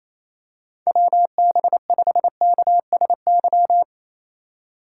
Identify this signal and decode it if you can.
Morse code, heard as a keyed tone, about 28 wpm, 700 Hz. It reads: WB5KSY
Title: USA Call Signs - 6 Characters - 4x Character Spacing - 28wpm